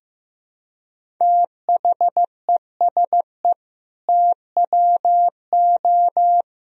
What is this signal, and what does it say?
Morse code, heard as a keyed tone, about 15 wpm, 700 Hz. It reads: THESE TWO
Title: Sentences from Top 1000 Words - Review Part 1 - 15wpm